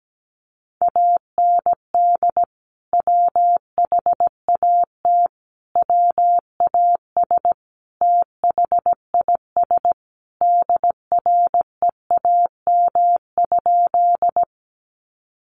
Morse code, heard as a keyed tone, about 17 wpm, 700 Hz. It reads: AND WHAT WAS THIS DREAM?
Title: Sentences from Top 1500 Words 17wpm